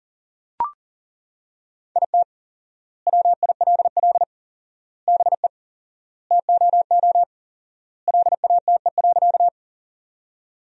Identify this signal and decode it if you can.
Morse code, heard as a keyed tone, about 40 wpm, 700 Hz. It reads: IT WILL BE TOO LATE.
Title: 4x Word-Spacing - Sentences from Top 300 Words - 40wpm